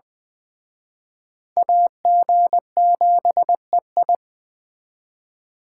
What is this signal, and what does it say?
Morse code, heard as a keyed tone, about 20 wpm, 700 Hz. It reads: AG7EI
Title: Speed Racing - USA 2x2 Call Signs 17wpm